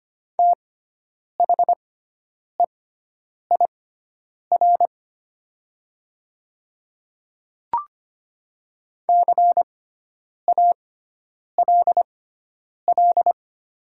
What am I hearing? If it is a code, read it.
Morse code, heard as a keyed tone, about 25 wpm, spaced as for 9 wpm, 700 Hz. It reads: THEIR CALL
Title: 6x Character Spacing - Top 100 Words - 25wpm